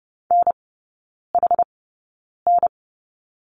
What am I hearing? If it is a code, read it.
Morse code, heard as a keyed tone, about 30 wpm, spaced as for 9 wpm, 700 Hz. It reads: NHN
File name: Lesson 16e - Letter c - Three Characters 30wpm